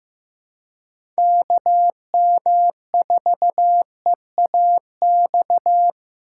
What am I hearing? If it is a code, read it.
Morse code, heard as a keyed tone, about 15 wpm, 700 Hz. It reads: KM4EAX
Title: USA Call Signs - 6 Characters - 8x Character Spacing - 15wpm